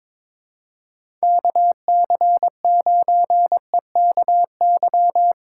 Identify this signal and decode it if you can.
Morse code, heard as a keyed tone, about 23 wpm, 700 Hz. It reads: KC9EKY
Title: USA Call Signs - 6 Characters - 3x Character Spacing - 22wpm